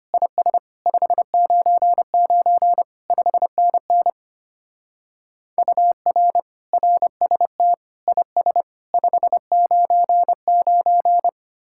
Morse code, heard as a keyed tone, about 32 wpm, 700 Hz. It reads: IS 599 5NN UR RST IS 599